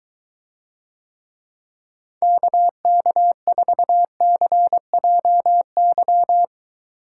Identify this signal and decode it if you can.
Morse code, heard as a keyed tone, about 22 wpm, 700 Hz. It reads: KK4CJY